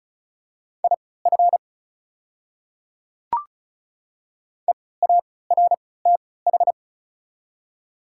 Morse code, heard as a keyed tone, about 34 wpm, spaced as for 20 wpm, 700 Hz. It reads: IF EARTH
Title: 3x Character Spacing - Top 200 Words - 35wpm